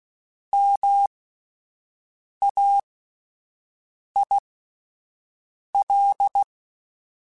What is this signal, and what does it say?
Morse code, heard as a keyed tone, about 16 wpm, spaced as for 5 wpm, 800 Hz. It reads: MAIL
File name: Alphabetical / Alphabetical mp3s / Ms @5